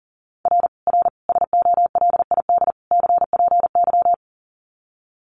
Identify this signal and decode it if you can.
Morse code, heard as a keyed tone, about 40 wpm, 700 Hz. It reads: R R SOLID CPY
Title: Common QSO Phrases 40wpm